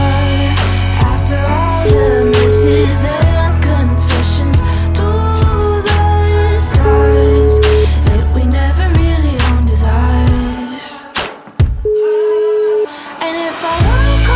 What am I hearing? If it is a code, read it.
Ringback tone (Europe, 425 Hz), with noise.